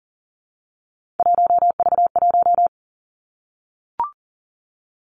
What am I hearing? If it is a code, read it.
Morse code, heard as a keyed tone, about 38 wpm, 700 Hz. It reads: 1V1
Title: Three Challenging Letter-Number Combinations 40wpm